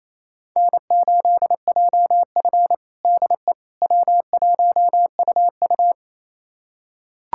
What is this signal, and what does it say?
Morse code, heard as a keyed tone, about 28 wpm, 700 Hz. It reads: N8JF DE W1UU
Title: USA Callsign DE Callsign - 4 Characters - 28wpm